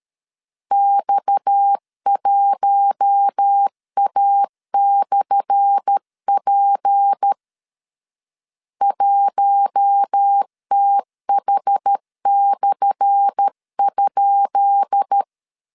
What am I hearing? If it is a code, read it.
Morse code, heard as a keyed tone, about 13 wpm, 800 Hz. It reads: X1A/P 1TH/?
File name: LEZIONI / 10 SIGNS / 10B INT 12wpm 2414